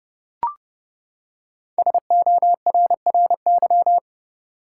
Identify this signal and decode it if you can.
Morse code, heard as a keyed tone, about 30 wpm, 700 Hz. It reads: SORRY
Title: Five Letter Words - 30wpm